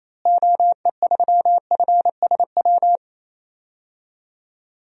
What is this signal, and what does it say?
Morse code, heard as a keyed tone, about 27 wpm, 700 Hz. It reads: OE3FSW
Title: DXCC-100 Call Signs - 6 Characters - Repeated 3x - 28wpm